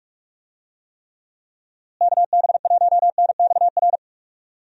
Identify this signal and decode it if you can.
Morse code, heard as a keyed tone, about 46 wpm, 700 Hz. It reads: KB1NXR
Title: USA Call Signs - 6 Characters - Repeated 3x - 45wpm